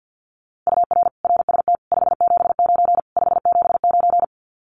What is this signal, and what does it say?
Morse code, heard as a keyed tone, about 48 wpm, 700 Hz. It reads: UR RST 579 579
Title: Common QSO Phrases Encore 50wpm